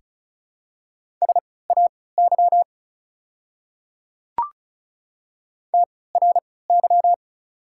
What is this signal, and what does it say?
Morse code, heard as a keyed tone, about 36 wpm, spaced as for 20 wpm, 690 Hz. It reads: SAY TRY
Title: 3x Character Spacing - Top 200 Words - 35wpm